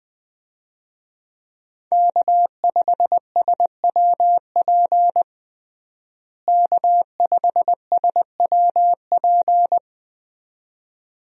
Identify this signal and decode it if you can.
Morse code, heard as a keyed tone, about 20 wpm, 700 Hz. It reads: K5SWP K5SWP
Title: USA Call Signs - 5 Characters - Repeated 2x - 20wpm